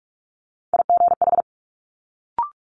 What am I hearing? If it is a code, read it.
Morse code, heard as a keyed tone, about 46 wpm, 700 Hz. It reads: IGH
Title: Multi-letter Phonemes Quick Fire 45wpm